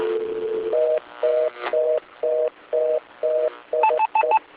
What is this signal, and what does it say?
Telephone call progress tone (North America), heard as a telephone network's tone, about 480, 620 Hz, with noise. Reorder tone.